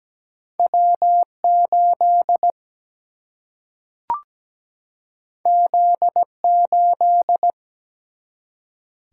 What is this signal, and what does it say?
Morse code, heard as a keyed tone, about 17 wpm, 700 Hz. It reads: W8 Z8
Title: Single-Letter Single-Number 17wpm